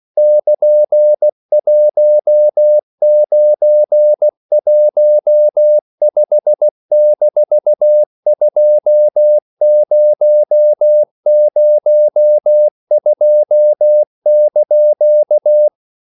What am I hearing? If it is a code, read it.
Morse code, heard as a keyed tone, about 16 wpm, 600 Hz. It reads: <KN>1915-2002)
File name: CW / Practice / EnglishQuotations0116